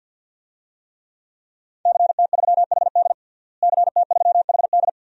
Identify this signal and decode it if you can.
Morse code, heard as a keyed tone, about 50 wpm, 700 Hz. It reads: CT3HD CT3HD
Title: DXCC-100 Call Signs - 5 Characters - Repeated 2x - 50wpm